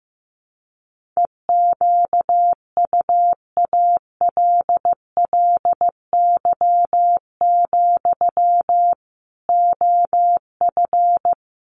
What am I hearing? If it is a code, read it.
Morse code, heard as a keyed tone, about 15 wpm, 700 Hz. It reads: EQUALLY, OF
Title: Sentences from Top 1400 Words 15wpm